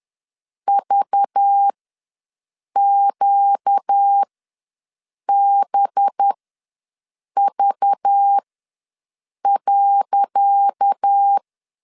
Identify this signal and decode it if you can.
Morse code, heard as a keyed tone, about 10 wpm, spaced as for 6 wpm, 800 Hz. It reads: VQBV.
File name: LEZIONI / 10 SIGNS / 10C DOT ws 10wpm 4213